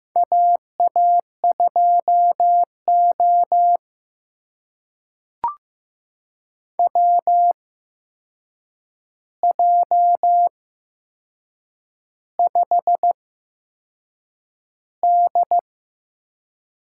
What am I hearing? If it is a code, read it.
Morse code, heard as a keyed tone, about 15 wpm, spaced as for 6 wpm, 700 Hz. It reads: AA2O WJ5D
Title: USA Call Signs - 8x Character Spacing - 15wpm